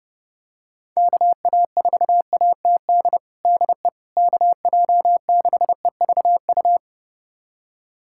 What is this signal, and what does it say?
Morse code, heard as a keyed tone, about 30 wpm, 700 Hz. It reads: KA4ATD DE KJ6EVU
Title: USA Callsign DE Callsign - 6 Characters - 30wpm